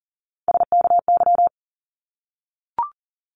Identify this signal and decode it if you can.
Morse code, heard as a keyed tone, about 39 wpm, 700 Hz. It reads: SKY
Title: Multi-letter Phonemes 40wpm